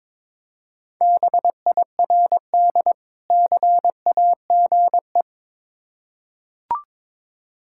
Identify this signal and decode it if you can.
Morse code, heard as a keyed tone, about 22 wpm, 690 Hz. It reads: BIRD CAGE
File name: Difficult Words with Hints - 22wpm